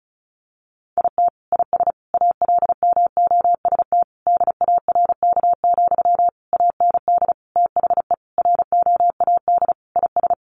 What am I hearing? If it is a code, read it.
Morse code, heard as a keyed tone, about 34 wpm, 700 Hz. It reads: IT IS ALMOST DARK, AND THE ROAD IS
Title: Sentences from Top 1200 Words 35wpm